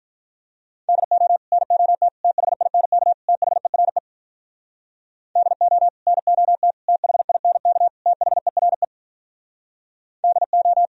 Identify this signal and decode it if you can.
Morse code, heard as a keyed tone, about 46 wpm, 700 Hz. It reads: DO NOT THINK THERE DO NOT THINK THERE DO